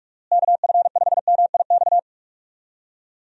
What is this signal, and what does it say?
Morse code, heard as a keyed tone, about 46 wpm, 700 Hz. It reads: KF5MIX
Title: USA Call Signs - 6 Characters - Repeated 3x - 45wpm